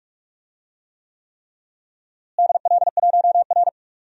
Morse code, heard as a keyed tone, about 48 wpm, 700 Hz. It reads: DL1R
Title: DXCC-100 Call Signs - 4 Characters - Repeated 3x - 45wpm